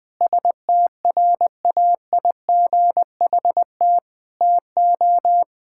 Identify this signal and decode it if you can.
Morse code, heard as a keyed tone, about 20 wpm, 700 Hz. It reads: STRAIGHT TO
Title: Sentences from Top 1000 Words - Review Part 6 - 20wpm